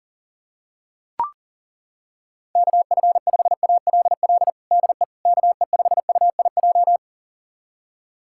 Morse code, heard as a keyed tone, about 40 wpm, 700 Hz. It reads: KF5ALL DE KE5UIJ
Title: USA Callsign DE Callsign 40wpm